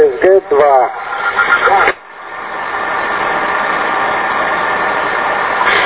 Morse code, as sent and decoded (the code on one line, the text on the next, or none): none